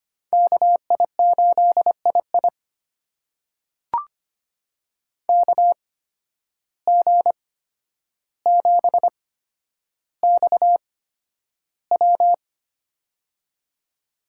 Morse code, heard as a keyed tone, about 25 wpm, spaced as for 11 wpm, 700 Hz. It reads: KI8II KG7XW
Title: USA Call Signs - 5 Characters - 8x Character Spacing - 25wpm